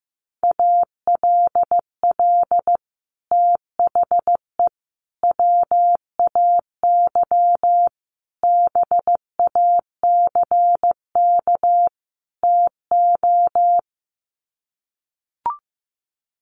Morse code, heard as a keyed tone, about 15 wpm, 700 Hz. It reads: ALL THE WAY BACK TO
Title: Sets of 5 Words Encore 15wpm